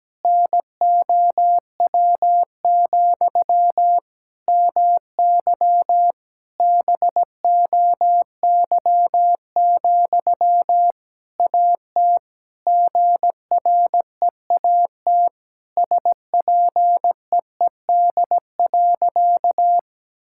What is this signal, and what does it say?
Morse code, heard as a keyed tone, about 17 wpm, 700 Hz. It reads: NOW, MY BOY, AT GREAT SPEED.
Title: Sentences from Top 1000 Words - Review Part 5 - 17wpm